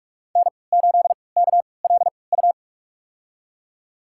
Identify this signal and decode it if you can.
Morse code, heard as a keyed tone, about 46 wpm, spaced as for 26 wpm, 700 Hz. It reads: N8KLU
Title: USA Call Signs - 5 Characters - 3x Character Spacing - 45wpm